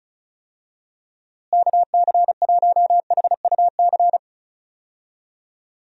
Morse code, heard as a keyed tone, about 36 wpm, 700 Hz. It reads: KC1HUC
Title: USA Call Signs - 6 Characters - Repeated 2x - 35wpm